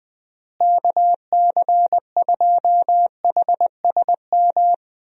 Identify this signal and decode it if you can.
Morse code, heard as a keyed tone, about 20 wpm, 700 Hz. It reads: KC2HSM
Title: USA Call Signs - 6 Characters - Repeated 2x - 20wpm